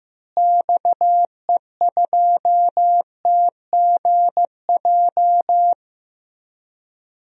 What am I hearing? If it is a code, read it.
Morse code, heard as a keyed tone, about 15 wpm, 700 Hz. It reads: XE2TGJ
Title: DXCC-100 Call Signs - 6 Characters - Repeated 2x - 15wpm